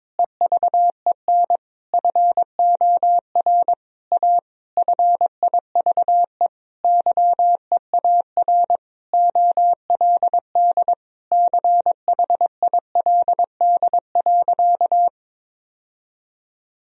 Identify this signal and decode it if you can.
Morse code, heard as a keyed tone, about 22 wpm, 700 Hz. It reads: EVEN FOR A FIVE YEAR OLD CHILD.
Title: Sentences from Top 1000 Words - Review Part 2 - 22wpm